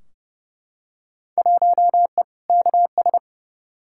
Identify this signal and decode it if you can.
Morse code, heard as a keyed tone, about 30 wpm, 700 Hz. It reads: 1E KS
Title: Field Day Exchanges - 30wpm